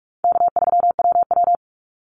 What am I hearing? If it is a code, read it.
Morse code, heard as a keyed tone, about 46 wpm, 700 Hz. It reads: K3WW
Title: Easy Call Signs - 45wpm